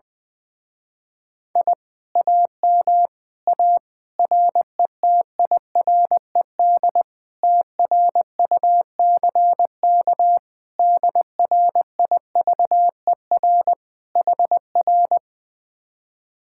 Morse code, heard as a keyed tone, about 20 wpm, 700 Hz. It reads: I AM A RETIRED TRUCK DRIVER HR
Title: Speed-Racing Common QSO Phrases - 20wpm